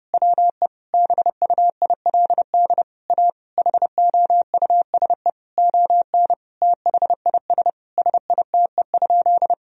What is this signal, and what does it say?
Morse code, heard as a keyed tone, about 30 wpm, 700 Hz. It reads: WE BUILD A HOUSE ON THIS SITE?
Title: Sentences from Top 1000 Words - Review Part 4 - 30wpm